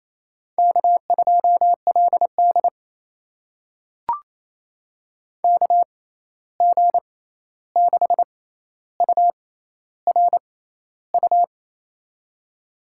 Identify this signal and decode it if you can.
Morse code, heard as a keyed tone, about 28 wpm, spaced as for 13 wpm, 700 Hz. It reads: K2LD KG6URU
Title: USA Call Signs - 6x Character Spacing - 28wpm